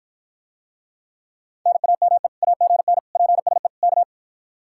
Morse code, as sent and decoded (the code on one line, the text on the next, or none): -. .- -- . / .- --. -. / .--. ... / -.-
NAME AGN PSE K